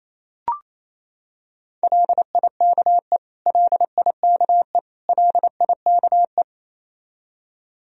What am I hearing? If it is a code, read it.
Morse code, heard as a keyed tone, about 28 wpm, 700 Hz. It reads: LIKE LIKE LIKE
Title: Top 100 Words Repeated 28wpm